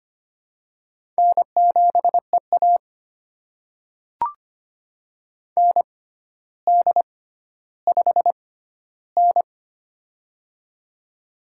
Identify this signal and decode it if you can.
Morse code, heard as a keyed tone, about 25 wpm, spaced as for 13 wpm, 700 Hz. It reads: N7EA ND5N
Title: USA Call Signs - 4 Characters - 6x Character Spacing - 25wpm